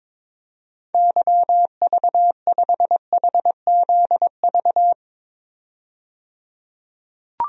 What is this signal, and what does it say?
Morse code, heard as a keyed tone, about 22 wpm, 700 Hz. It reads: YV5HZV
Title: DXCC-100 Call Signs - 6 Characters - 22wpm